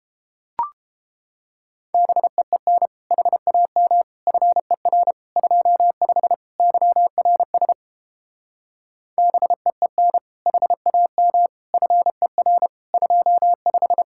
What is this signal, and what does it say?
Morse code, heard as a keyed tone, about 31 wpm, 700 Hz. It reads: BEEN HAM FER 25 YRS BEEN HAM FER 25